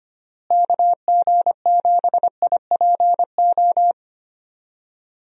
Morse code, heard as a keyed tone, about 25 wpm, 700 Hz. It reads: KG7IPO
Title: USA Call Signs - 6 Characters - Repeated 2x - 25wpm